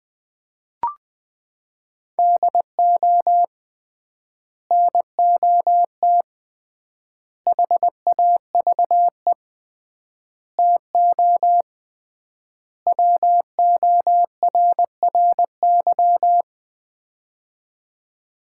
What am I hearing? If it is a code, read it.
Morse code, heard as a keyed tone, about 20 wpm, 700 Hz. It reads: DO NOT HAVE TO WORRY
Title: 3x Word-Spacing - Sets of 5 Words - 20wpm